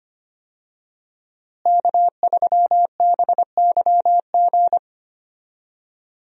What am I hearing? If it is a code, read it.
Morse code, heard as a keyed tone, about 25 wpm, 700 Hz. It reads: K3BYG
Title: USA Call Signs - 5 Characters - Repeated 3x - 25wpm